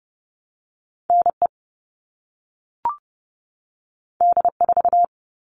Morse code, heard as a keyed tone, about 30 wpm, 700 Hz. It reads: NE D4
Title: All Cut Numbers - 2-Digit 30wpm